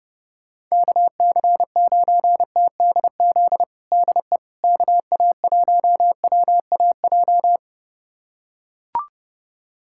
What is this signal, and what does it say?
Morse code, heard as a keyed tone, about 30 wpm, 700 Hz. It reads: KC9TDZ DE KA1WAJ
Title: USA Callsign DE Callsign - 6 Characters - 30wpm